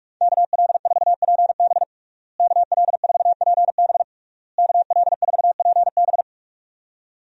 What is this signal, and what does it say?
Morse code, heard as a keyed tone, about 42 wpm, 700 Hz. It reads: KL4PB KL4PB KL4PB
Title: USA Call Signs - 5 Characters - Repeated 3x - 45wpm